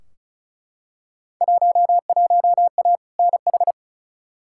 Morse code, heard as a keyed tone, about 34 wpm, 700 Hz. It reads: 11A NH
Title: Field Day Exchanges Encore - 35wpm